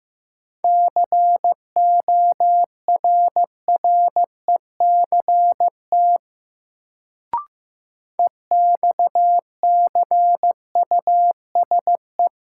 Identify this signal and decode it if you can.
Morse code, heard as a keyed tone, about 15 wpm, 700 Hz. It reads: CORRECT EXCUSE
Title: Sentences from Top 700 Words - New Word Drill - 15wpm